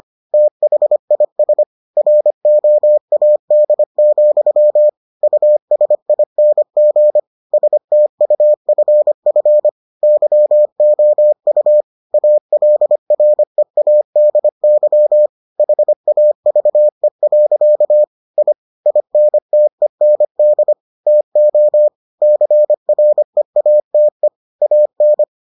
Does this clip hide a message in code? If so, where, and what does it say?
Yes: Morse code from 0.3 to 25.2 s, heard as a keyed tone, about 25 wpm, 600 Hz. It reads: THIS ROAD, USING STUFF YOU ALREADY HAVE. I INTEND TO CREATE AN